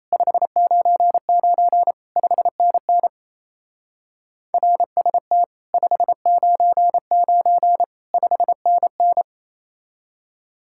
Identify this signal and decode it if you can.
Morse code, heard as a keyed tone, about 33 wpm, 700 Hz. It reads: 599 5NN RST 599 5NN
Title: Speed-Racing Common QSO Phrases - 28wpm